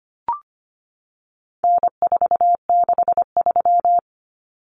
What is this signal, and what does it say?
Morse code, heard as a keyed tone, about 25 wpm, 700 Hz. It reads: N463
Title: Common Cut Numbers - 4-Digit 25wpm